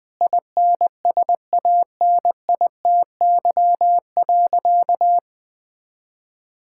Morse code, heard as a keyed tone, about 21 wpm, 700 Hz. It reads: INSANITY.
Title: Sentences with 8-Word Phrases - 20wpm